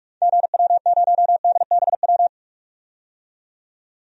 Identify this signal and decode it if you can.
Morse code, heard as a keyed tone, about 47 wpm, 700 Hz. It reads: GW0DBW
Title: DXCC-100 Call Signs - 6 Characters - 45wpm